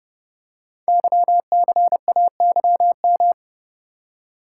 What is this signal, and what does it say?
Morse code, heard as a keyed tone, about 30 wpm, 700 Hz. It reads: YCAYM
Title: Code Groups - 5 Letters - 30wpm